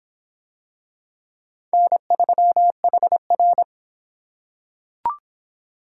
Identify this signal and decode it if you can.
Morse code, heard as a keyed tone, about 25 wpm, 700 Hz. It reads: N3HR